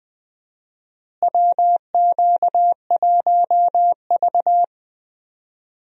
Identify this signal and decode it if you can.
Morse code, heard as a keyed tone, about 20 wpm, 700 Hz. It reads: WQ1V
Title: USA Call Signs - 4 Characters - Repeated 3x - 20wpm